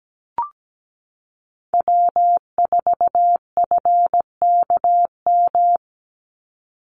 Morse code, heard as a keyed tone, about 17 wpm, 700 Hz. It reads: W4FKM
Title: Call Signs Repeated 3x - 17wpm